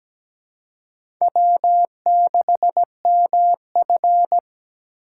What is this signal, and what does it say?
Morse code, heard as a keyed tone, about 17 wpm, 700 Hz. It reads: W6MF
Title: USA Call Signs - 6x Character Spacing - 17wpm